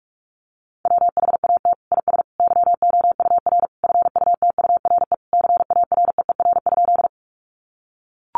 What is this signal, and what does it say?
Morse code, heard as a keyed tone, about 44 wpm, 700 Hz. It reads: WHAT IS YOUR FUTURE CAREER?
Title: Sentences from Top 1000 Words 45wpm